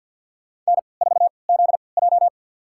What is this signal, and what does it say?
Morse code, heard as a keyed tone, about 52 wpm, spaced as for 29 wpm, 700 Hz. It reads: N4ZJ
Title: USA Call Signs - 4 Characters - 3x Character Spacing - 50wpm